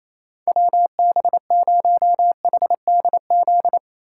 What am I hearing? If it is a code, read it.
Morse code, heard as a keyed tone, about 35 wpm, 700 Hz. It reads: WB0HDZ
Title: Speed Racing - USA 2x3 Call Signs 28wpm